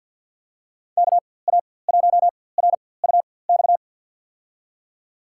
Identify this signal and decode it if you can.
Morse code, heard as a keyed tone, about 51 wpm, spaced as for 23 wpm, 700 Hz. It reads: KA1RUX
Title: USA Call Signs - 6 Characters - 4x Character Spacing - 50wpm